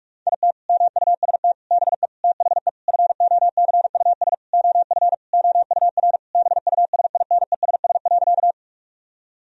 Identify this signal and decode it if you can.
Morse code, heard as a keyed tone, about 46 wpm, 700 Hz. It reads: IT MUST BE THE FOCUS OF OUR BUSINESS.
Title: Sentences from Top 1000 Words - Review Part 4 - 45wpm